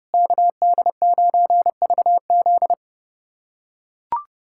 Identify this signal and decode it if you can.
Morse code, heard as a keyed tone, about 32 wpm, 700 Hz. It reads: KD9VZ
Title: USA Callsign DE Callsign - 5 Characters - 30wpm